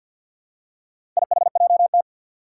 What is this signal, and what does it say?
Morse code, heard as a keyed tone, about 51 wpm, 700 Hz. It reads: IHJT
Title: Four Challenging Letter-Number Combinations 50wpm